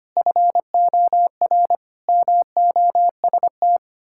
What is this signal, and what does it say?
Morse code, heard as a keyed tone, about 25 wpm, 700 Hz. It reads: FOR MOST
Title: Pangram Challenge v2 - 25wpm